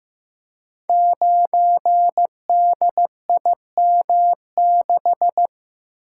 Morse code, heard as a keyed tone, about 15 wpm, 700 Hz. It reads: 9DIM6
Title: Code Groups - 5 Letter:Numbers - 15wpm